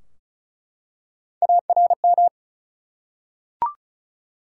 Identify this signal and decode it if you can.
Morse code, heard as a keyed tone, about 36 wpm, 700 Hz. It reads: ARM
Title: Three Letter Words - 35wpm